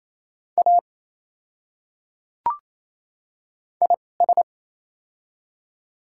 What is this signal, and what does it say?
Morse code, heard as a keyed tone, about 28 wpm, spaced as for 20 wpm, 700 Hz. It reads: A IS
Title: 2x Character Spacing - Top 100 Words - 28wpm